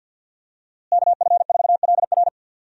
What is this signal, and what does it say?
Morse code, heard as a keyed tone, about 51 wpm, 700 Hz. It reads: KF4LR
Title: USA Call Signs - 5 Characters - Repeated 3x - 50wpm